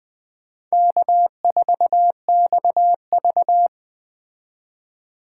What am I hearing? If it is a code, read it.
Morse code, heard as a keyed tone, about 21 wpm, 700 Hz. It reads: K4XV